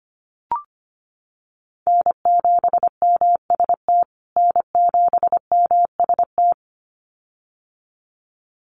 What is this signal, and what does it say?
Morse code, heard as a keyed tone, about 25 wpm, 700 Hz. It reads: N7MST N7MST
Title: USA Call Signs Repeated 2x 25wpm